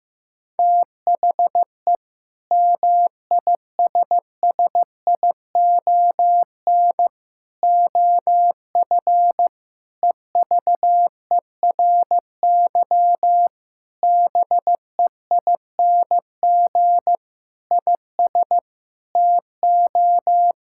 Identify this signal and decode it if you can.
Morse code, heard as a keyed tone, about 15 wpm, 700 Hz. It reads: THE MISSION OF EVERY BEING IS TO
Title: Sentences from Top 1600 Words 15wpm